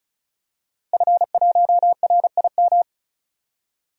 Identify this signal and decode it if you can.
Morse code, heard as a keyed tone, about 36 wpm, 700 Hz. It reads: F1RIM
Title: DXCC-100 Call Signs - 5 Characters - Repeated 2x - 35wpm